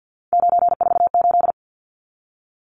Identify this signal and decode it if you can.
Morse code, heard as a keyed tone, about 48 wpm, 700 Hz. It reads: 948
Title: Numbers - 3-Digit 50wpm